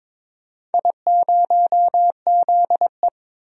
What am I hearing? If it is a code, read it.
Morse code, heard as a keyed tone, about 22 wpm, 700 Hz. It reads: I0ZE
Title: DXCC-100 Call Signs - 4 Characters - 22wpm